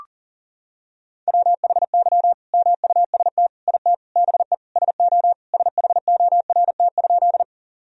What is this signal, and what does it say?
Morse code, heard as a keyed tone, about 40 wpm, 700 Hz. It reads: WHY MUST IT BE SO SHORT?
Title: Sentences from Top 1000 Words - Review Part 2 - 40wpm